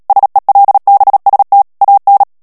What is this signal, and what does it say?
Morse code, heard as a keyed tone, about 37 wpm, 800 Hz. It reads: SELBST AN
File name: morse mp3 / ALLEMAND texte clair 2 vitesse 35 mots minute